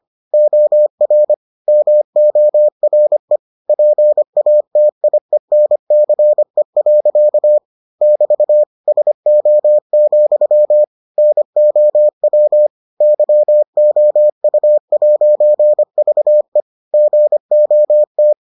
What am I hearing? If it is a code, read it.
Morse code, heard as a keyed tone, about 25 wpm, 600 Hz. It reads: OR MORE PATIENCE. <BT> SO, NOW YOU'VE GOT